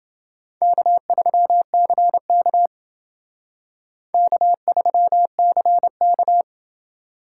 Morse code, heard as a keyed tone, about 28 wpm, 700 Hz. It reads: K3CK K3CK